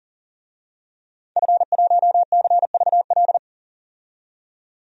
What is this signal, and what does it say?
Morse code, heard as a keyed tone, about 40 wpm, 700 Hz. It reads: F1CVL
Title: DXCC-100 Call Signs - 5 Characters - 40wpm